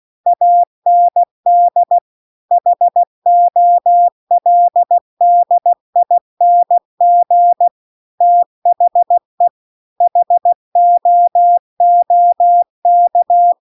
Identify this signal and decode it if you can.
Morse code, heard as a keyed tone, about 16 wpm, 700 Hz. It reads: AND HOLDING THE HOOK